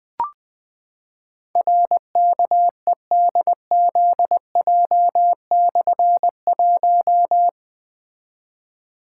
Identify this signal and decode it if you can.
Morse code, heard as a keyed tone, about 21 wpm, 700 Hz. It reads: RKEDZJ/1
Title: Difficult Call Signs 20wpm